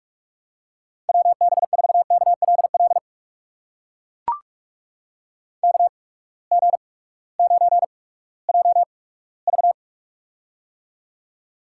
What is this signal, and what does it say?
Morse code, heard as a keyed tone, about 44 wpm, 690 Hz. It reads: W B 4 K L L K G 9 J V
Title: USA Call Signs - 8x Character Spacing - 45wpm